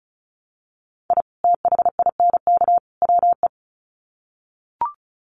Morse code, heard as a keyed tone, about 34 wpm, 700 Hz. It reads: I THINK WE
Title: Sets of 3 Words 35wpm